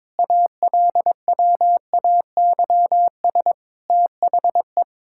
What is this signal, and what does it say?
Morse code, heard as a keyed tone, about 22 wpm, 700 Hz. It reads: ALWAYS THE
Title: Sentences from Top 1000 Words - Review Part 5 - 22wpm